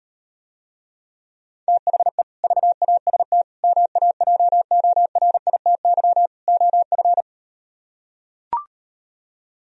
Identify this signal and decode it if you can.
Morse code, heard as a keyed tone, about 38 wpm, 700 Hz. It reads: THE VAST MAJORITY OF